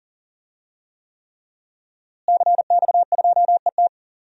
Morse code, heard as a keyed tone, about 40 wpm, 700 Hz. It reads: CX2ET
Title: DXCC-100 Call Signs - 5 Characters - Repeated 3x - 40wpm